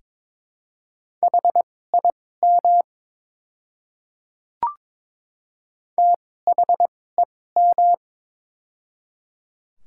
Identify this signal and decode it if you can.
Morse code, heard as a keyed tone, about 23 wpm, spaced as for 16 wpm, 700 Hz. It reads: HIM THEM
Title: 2x Character Spacing - Top 200 Words - 22wpm